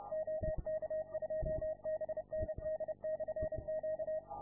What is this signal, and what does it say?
Morse code, heard as a keyed tone, about 30 wpm, 630 Hz. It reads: OK2BND/Q